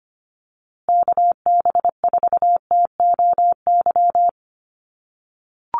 Morse code, heard as a keyed tone, about 25 wpm, 700 Hz. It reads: KB4TOY
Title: USA Call Signs Repeated 2x 25wpm